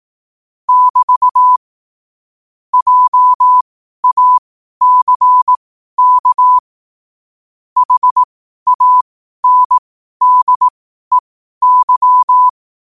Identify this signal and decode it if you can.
Morse code, heard as a keyed tone, about 18 wpm, spaced as for 12 wpm, 1000 Hz. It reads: <BT> JACK HANDEY
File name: qotdfiles / 12wpm